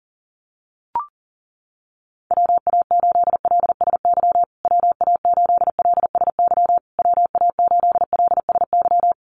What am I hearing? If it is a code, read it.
Morse code, heard as a keyed tone, about 40 wpm, 700 Hz. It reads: WA8LSY WA8LSY WA8LSY
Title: USA Call Signs Repeated 3x 40wpm